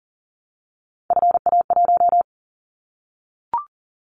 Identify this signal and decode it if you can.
Morse code, heard as a keyed tone, about 39 wpm, 700 Hz. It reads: FA1
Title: Callsign Prefixes for Top 50 DXCC - 40wpm